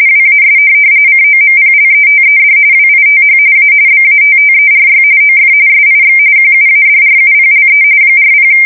RTTY, about 45.45 baud. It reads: WAS COMPLETED WITH A BATTERY OF SIX QUADRUPLE 40